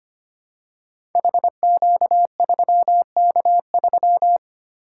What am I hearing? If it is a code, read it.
Morse code, heard as a keyed tone, about 25 wpm, 700 Hz. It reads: HQ3K3
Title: Code Groups - 5 Letter:Numbers - 25wpm